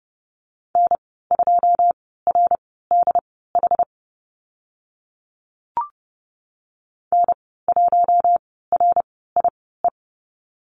Morse code, heard as a keyed tone, about 30 wpm, spaced as for 17 wpm, 700 Hz. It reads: N2RDH N1RIE
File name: Lesson 14g - Number 2 - Call Signs 30wpm